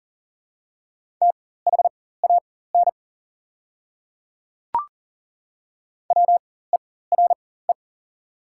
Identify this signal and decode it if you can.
Morse code, heard as a keyed tone, about 39 wpm, spaced as for 19 wpm, 700 Hz. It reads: THAN WERE
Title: 4x Character Spacing - Top 300 Words - 40wpm